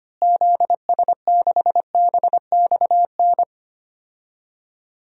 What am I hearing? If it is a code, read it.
Morse code, heard as a keyed tone, about 25 wpm, 700 Hz. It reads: ZS6BXN
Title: DXCC-100 Call Signs - 6 Characters - Repeated 3x - 25wpm